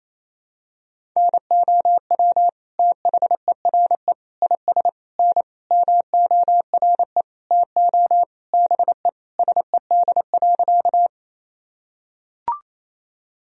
Morse code, heard as a keyed tone, about 28 wpm, 700 Hz. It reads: NOW THERE IS N MORE TO BE SED.
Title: Sentences from Top 100 Words and Abbreviations 28wpm